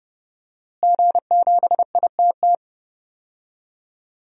Morse code, heard as a keyed tone, about 30 wpm, 700 Hz. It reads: G7ITT
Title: DXCC-100 Call Signs - 5 Characters - Repeated 2x - 30wpm